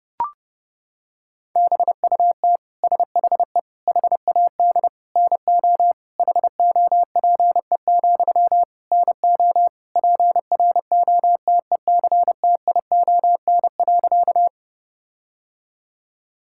Morse code, heard as a keyed tone, about 32 wpm, 700 Hz. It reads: BUT SHE HAD NO HOPE, NO PROTECTION.
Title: Sentences from Top 1600 Words 30wpm